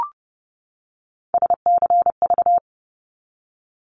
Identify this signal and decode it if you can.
Morse code, heard as a keyed tone, about 30 wpm, 700 Hz. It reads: SCV
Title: ARRL Sections 30wpm